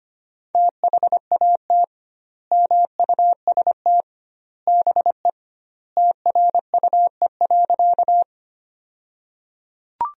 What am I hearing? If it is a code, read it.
Morse code, heard as a keyed tone, about 25 wpm, 700 Hz. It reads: THAT MUST BE TRUE.
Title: 2x Word-Spacing - Sentences from Top 400 Words - 25wpm